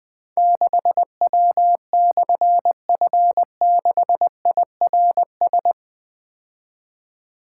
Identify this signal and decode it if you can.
Morse code, heard as a keyed tone, about 20 wpm, 700 Hz. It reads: 6W/F6IRS
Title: Difficult Call Signs 20wpm